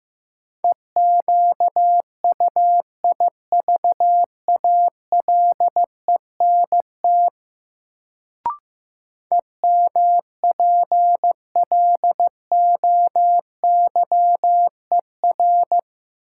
Morse code, heard as a keyed tone, about 15 wpm, 700 Hz. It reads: EQUIVALENT EMPLOYER
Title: Sentences from Top 1600 Words - New Word Drill - 15wpm